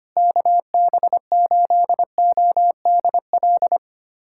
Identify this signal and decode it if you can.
Morse code, heard as a keyed tone, about 25 wpm, 700 Hz. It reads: KB8ODL
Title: USA Callsign DE Callsign 25wpm